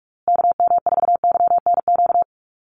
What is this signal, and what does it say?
Morse code, heard as a keyed tone, about 44 wpm, 700 Hz. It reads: KM4YNQ
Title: USA Call Signs Repeated 2x 45wpm